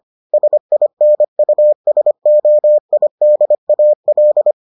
Morse code, heard as a keyed tone, about 25 wpm, 600 Hz. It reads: SINUSOIDAL